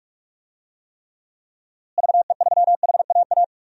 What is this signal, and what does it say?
Morse code, heard as a keyed tone, about 45 wpm, 700 Hz. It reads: VE3HAA
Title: DXCC-100 Call Signs - 6 Characters - Repeated 2x - 45wpm